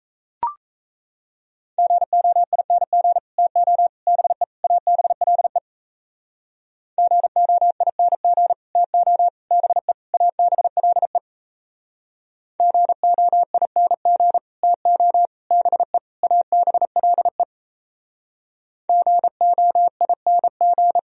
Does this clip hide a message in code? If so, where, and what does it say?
Yes: Morse code from 1.8 to 21.0 s, heard as a keyed tone, about 35 wpm, 700 Hz. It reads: GOING TO BE ABLE GOING TO BE ABLE GOING TO BE ABLE GOING